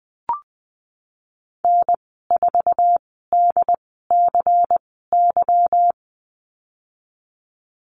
Morse code, heard as a keyed tone, about 20 wpm, spaced as for 14 wpm, 700 Hz. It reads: N4DCY
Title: 147 Lesson 24g - Letter y - Call Signs 20wpm